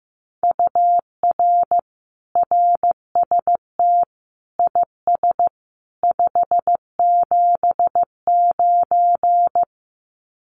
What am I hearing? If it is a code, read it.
Morse code, heard as a keyed tone, about 15 wpm, 700 Hz. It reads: UR RST IS 579
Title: Common QSO Phrases 15wpm